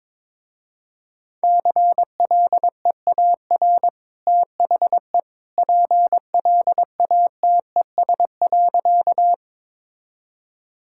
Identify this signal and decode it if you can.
Morse code, heard as a keyed tone, about 22 wpm, 700 Hz. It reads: CLEAR THE PLATES.
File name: practice / 22wpm / Easy Sentences with Hints - 22wpm